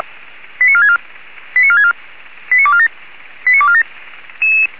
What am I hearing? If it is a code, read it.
Five-tone selective call (ZVEI), heard as quick runs of five tones, tones of about 70 ms. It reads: 89464 89464 89247 89247